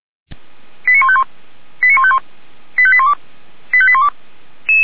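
Five-tone selective call (ZVEI), heard as quick runs of five tones, tones of about 70 ms. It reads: 89151 89151 85812 85812
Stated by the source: Five-tone ZVEI selective call 89123, received by SDR, recorded for the Feuerwehr Orlamünde fire brigade.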